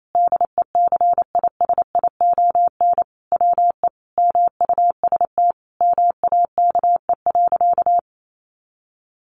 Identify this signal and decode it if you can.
Morse code, heard as a keyed tone, about 28 wpm, 700 Hz. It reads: DECISION WE MUST MAKE.
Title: Sentences from Top 1000 Words 28wpm